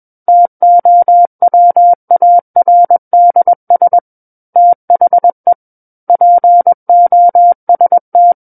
Morse code, heard as a keyed tone, about 21 wpm, 700 Hz. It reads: TOWARDS THE POST
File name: Book One / 12-Chapter 12 - 21 WPM